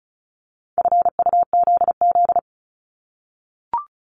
Morse code, heard as a keyed tone, about 34 wpm, 700 Hz. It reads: FUZZ
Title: Multi-letter Phonemes Repeated 3x 35wpm